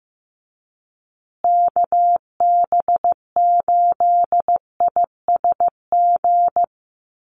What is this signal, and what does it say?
Morse code, heard as a keyed tone, about 15 wpm, 700 Hz. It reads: KB8ISG
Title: USA Call Signs Repeated 2x 15wpm